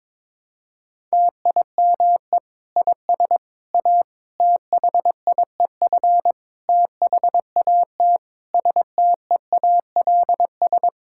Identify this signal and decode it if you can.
Morse code, heard as a keyed tone, about 21 wpm, 700 Hz. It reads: TIME IS A THIEF THAT STEALS
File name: Difficult Sentences with Hints - 22wpm